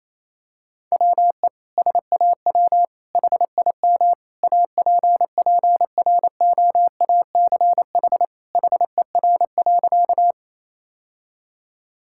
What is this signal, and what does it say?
Morse code, heard as a keyed tone, about 28 wpm, 700 Hz. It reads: WE SAW HIM APPROACH HER.
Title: Sentences from Top 1000 Words - Review Part 6 - 28wpm